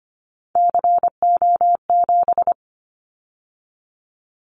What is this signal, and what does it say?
Morse code, heard as a keyed tone, about 25 wpm, 700 Hz. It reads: CO7
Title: Callsign Prefixes for Top 100 DXCC - 25wpm